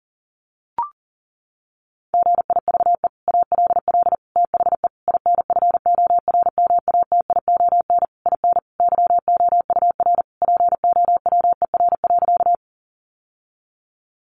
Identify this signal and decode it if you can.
Morse code, heard as a keyed tone, about 40 wpm, 700 Hz. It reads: GIVE ALL THE INFORMATION IN YOUR POWER.
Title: Sentences from Top 600 Words 40wpm